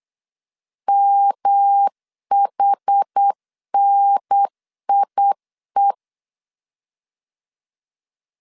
Morse code, heard as a keyed tone, about 8 wpm, 800 Hz. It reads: MHNIE
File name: CW / DET / 1 3 1500C S B 8wpm 4141